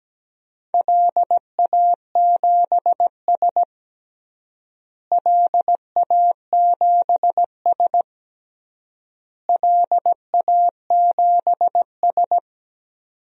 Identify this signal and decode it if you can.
Morse code, heard as a keyed tone, about 17 wpm, 700 Hz. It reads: LA7S LA7S LA7S
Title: DXCC-100 Call Signs - 4 Characters - Repeated 3x - 17wpm